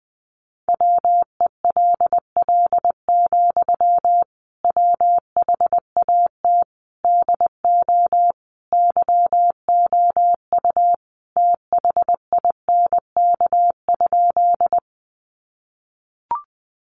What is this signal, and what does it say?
Morse code, heard as a keyed tone, about 20 wpm, 700 Hz. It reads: WELL, WHAT DO YOU THINK?
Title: Sentences from Top 200 Words 20wpm